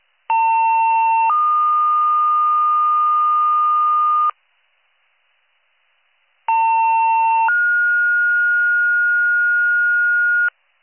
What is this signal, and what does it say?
Two-tone page, heard as a one-second tone, then a three-second tone, about 900 then 1220 Hz, about 900 then 1430 Hz. It is heard from 0.3 to 10.5 s.